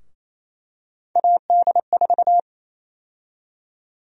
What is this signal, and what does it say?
Morse code, heard as a keyed tone, about 28 wpm, 700 Hz. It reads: AD4
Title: USA Call Sign Prefixes - 28wpm